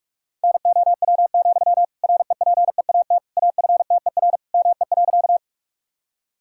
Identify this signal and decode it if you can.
Morse code, heard as a keyed tone, about 46 wpm, 700 Hz. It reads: NOW, REPEAT AFTER ME.
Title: Sentences from Top 1600 Words 45wpm